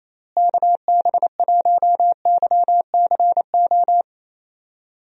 Morse code, heard as a keyed tone, about 28 wpm, 700 Hz. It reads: KB1YCO
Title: USA Call Signs - 6 Characters - Repeated 2x - 28wpm